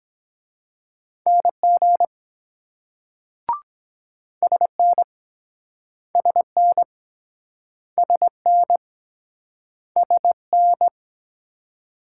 Morse code, heard as a keyed tone, about 24 wpm, 700 Hz. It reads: NG SN SN SN SN